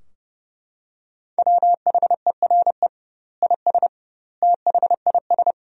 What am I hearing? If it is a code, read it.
Morse code, heard as a keyed tone, about 32 wpm, 700 Hz. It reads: WHERE IS THIS
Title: 2x Word-Spacing - Sentences from Top 300 Words - 30wpm